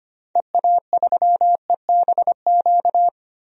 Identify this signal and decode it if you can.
Morse code, heard as a keyed tone, about 25 wpm, 700 Hz. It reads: EA3EBQ
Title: DXCC-100 Call Signs - 6 Characters - 25wpm